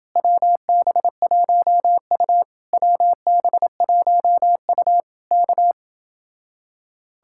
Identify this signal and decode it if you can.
Morse code, heard as a keyed tone, about 30 wpm, 700 Hz. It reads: WB1U WB1U K